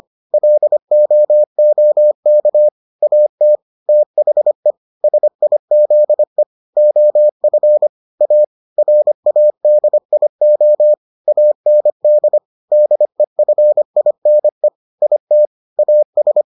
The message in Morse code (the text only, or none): LOOK AT THE SIZE OF A RADIO AND DEFINE IT AS